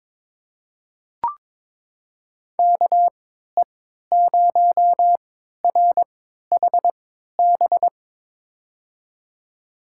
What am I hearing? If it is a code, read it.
Morse code, heard as a keyed tone, about 22 wpm, spaced as for 13 wpm, 700 Hz. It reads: KE0RHB
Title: USA Call Signs - 6 Characters - 3x Character Spacing - 22wpm